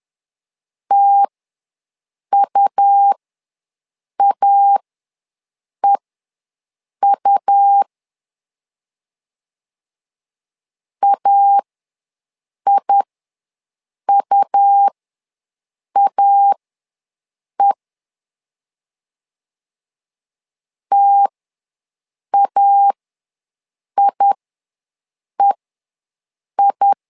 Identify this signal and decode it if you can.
Morse code, heard as a keyed tone, about 11 wpm, spaced as for 6 wpm, 800 Hz. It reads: TUAEU AIUAE TAIEI
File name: LEZIONI / 2 UHDN / 2 U ws 10wpm 2017